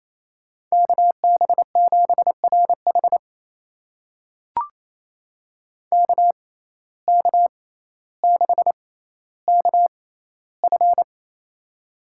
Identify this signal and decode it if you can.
Morse code, heard as a keyed tone, about 28 wpm, spaced as for 14 wpm, 700 Hz. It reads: KB7RH KK6KF
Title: USA Call Signs - 6x Character Spacing - 28wpm